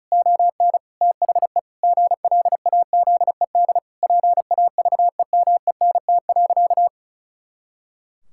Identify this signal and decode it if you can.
Morse code, heard as a keyed tone, about 34 wpm, 700 Hz. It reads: ON THE GLAZED PAVEMENT.
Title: Pangram Challenge v2 - 35wpm